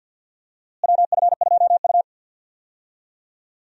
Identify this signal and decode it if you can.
Morse code, heard as a keyed tone, about 51 wpm, 700 Hz. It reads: WF2U
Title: USA Call Signs - 4 Characters - Repeated 3x - 50wpm